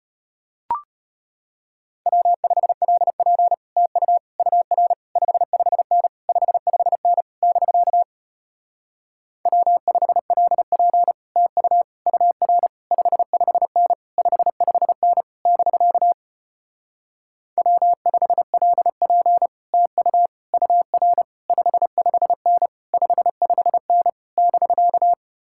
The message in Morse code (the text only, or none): W5LP TU UR 55N 55N <BK> W5LP TU UR 55N 55N <BK> W5LP TU UR 55N 55N <BK>